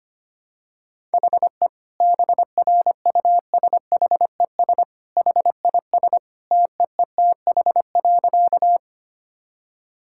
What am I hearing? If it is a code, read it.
Morse code, heard as a keyed tone, about 25 wpm, 700 Hz. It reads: HE BRUSHES HIS TEETH.
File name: Easy Sentences with Hints - 25wpm